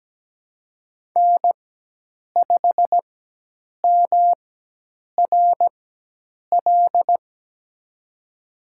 Morse code, heard as a keyed tone, about 17 wpm, spaced as for 8 wpm, 700 Hz. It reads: N5MRL
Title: USA Call Signs - 5 Characters - 4x Character Spacing - 17wpm